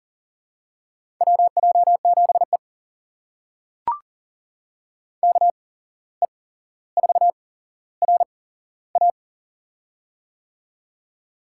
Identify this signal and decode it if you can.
Morse code, heard as a keyed tone, about 40 wpm, spaced as for 16 wpm, 700 Hz. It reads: WJ7E KE4RA
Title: USA Call Signs - 8x Character Spacing - 40wpm